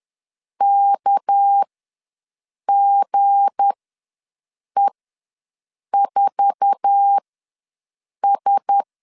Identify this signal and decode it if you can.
Morse code, heard as a keyed tone, about 10 wpm, spaced as for 6 wpm, 800 Hz. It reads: KGE4S
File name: LEZIONI / 9 28 ws 10wpm 4124